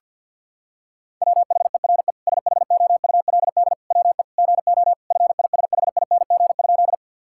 Morse code, heard as a keyed tone, about 49 wpm, 700 Hz. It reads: WHERE SHOULD WE GO FISHING?
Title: Sentences from Top 1000 Words - Review Part 5 - 50wpm